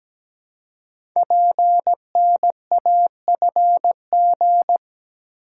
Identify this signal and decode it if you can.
Morse code, heard as a keyed tone, about 17 wpm, 700 Hz. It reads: PNAFG
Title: Code Groups - 5 Letters - 17wpm